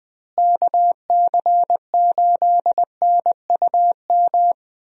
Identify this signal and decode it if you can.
Morse code, heard as a keyed tone, about 20 wpm, 700 Hz. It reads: KC8NUM
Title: USA Call Signs - 6 Characters - Repeated 3x - 20wpm